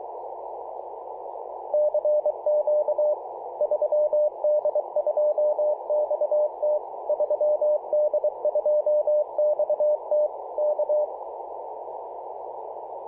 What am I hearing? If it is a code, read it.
Morse code, heard as a keyed tone, about 24 wpm, 600 Hz, with noise. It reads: CQ 3D2XT 3D2XT K
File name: posts / apr-13-2119z-3d2xt-21-021mhz-cw